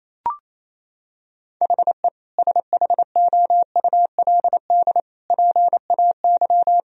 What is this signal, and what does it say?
Morse code, heard as a keyed tone, about 28 wpm, 700 Hz. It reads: HE SHOULD PAY
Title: Sentences from Top 1000 Words - Review Part 2 - 28wpm